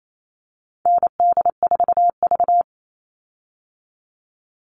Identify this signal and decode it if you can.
Morse code, heard as a keyed tone, about 28 wpm, 700 Hz. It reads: ND4V
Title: Call Signs Repeated 3x - 28wpm